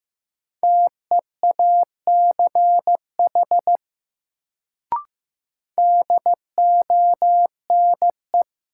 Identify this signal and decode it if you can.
Morse code, heard as a keyed tone, about 15 wpm, 700 Hz. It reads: TEACH DONE
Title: Sentences from Top 500 Words - New Word Drill - 15wpm